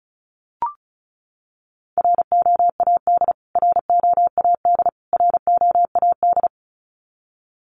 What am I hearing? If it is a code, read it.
Morse code, heard as a keyed tone, about 34 wpm, 700 Hz. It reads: ROAD ROAD ROAD
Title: Top 500 Words Repeated 35wpm